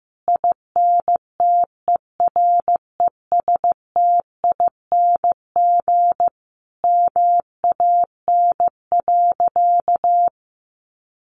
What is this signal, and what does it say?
Morse code, heard as a keyed tone, about 15 wpm, 700 Hz. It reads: INTERESTING MAN.
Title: Sentences from Top 1000 Words 15wpm